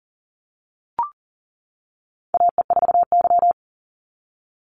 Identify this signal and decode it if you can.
Morse code, heard as a keyed tone, about 38 wpm, 700 Hz. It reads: AE4Y
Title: NA SOTA Chaser Callsigns 40wpm